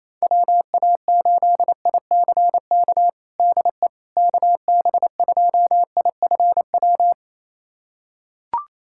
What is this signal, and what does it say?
Morse code, heard as a keyed tone, about 28 wpm, 700 Hz. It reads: WA8ICK DE KB2IFW
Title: USA Callsign DE Callsign 28wpm